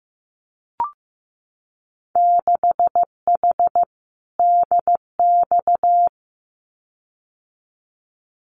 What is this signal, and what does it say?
Morse code, heard as a keyed tone, about 15 wpm, 700 Hz. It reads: 6H DX
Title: Winter Field Day Exchanges Encore 15wpm